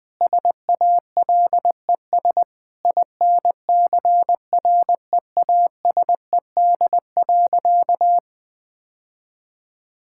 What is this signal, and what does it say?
Morse code, heard as a keyed tone, about 20 wpm, 700 Hz. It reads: SALES INCREASED.
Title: Sentences with 6-Word Phrases - Spoken Beginning - 20wpm